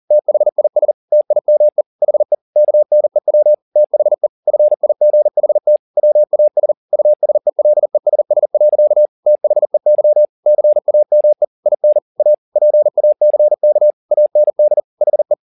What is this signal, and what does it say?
Morse code, heard as a keyed tone, about 40 wpm, 600 Hz. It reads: THIS TIME HE KNEW THE FIGHT WAS USELESS. THEY CAME IN A PACK AND HE
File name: the old man and the sea 2 40wpm